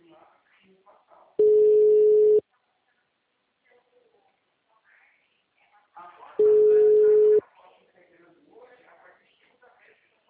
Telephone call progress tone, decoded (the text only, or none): ringback tone